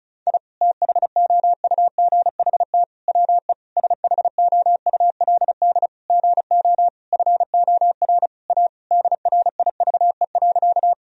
Morse code, heard as a keyed tone, about 36 wpm, 700 Hz. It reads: I THOUGHT WE SHOULD GO FOR A DRIVE.
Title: Sentences from Top 1000 Words - Review Part 2 - 35wpm